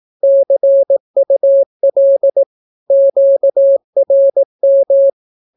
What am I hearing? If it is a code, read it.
Morse code, heard as a keyed tone, about 18 wpm, 550 Hz. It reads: CUL QRM
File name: naukapodstaw / skroty / skroty